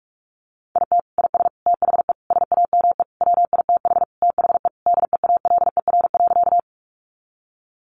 Morse code, heard as a keyed tone, about 44 wpm, 700 Hz. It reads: IT IS THE SAME WITH THE DEALER.
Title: Sentences from Top 1100 Words 45wpm